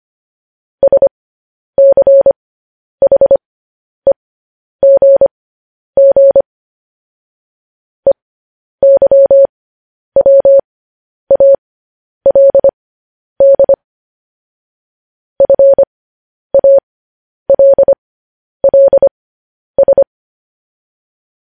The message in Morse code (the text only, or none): SCHEGG EYWALD FALLS